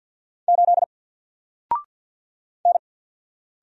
Morse code, heard as a keyed tone, about 49 wpm, 700 Hz. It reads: 8 N
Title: Single Character (Letter:Number:Punctuation:Prosign) 50wpm